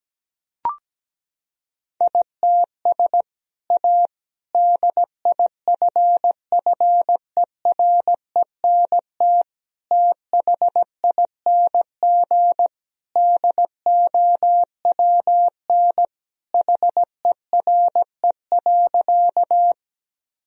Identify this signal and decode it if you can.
Morse code, heard as a keyed tone, about 17 wpm, 700 Hz. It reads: ITS A DIFFERENT THING DOWN HERE.
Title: Sentences from Top 1000 Words - Review Part 3 - 17wpm